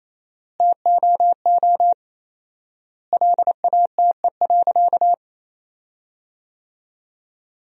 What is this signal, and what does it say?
Morse code, heard as a keyed tone, about 28 wpm, 700 Hz. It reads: TOO LATE.
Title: 4x Word-Spacing - Sentences from Top 300 Words - 28wpm